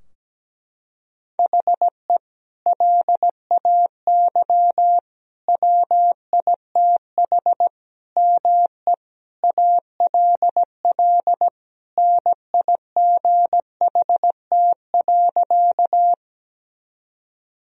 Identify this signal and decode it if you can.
Morse code, heard as a keyed tone, about 17 wpm, 700 Hz. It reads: HE LAY WITH ME ALL NIGHT.
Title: Sentences from Top 1000 Words - Review Part 2 - 17wpm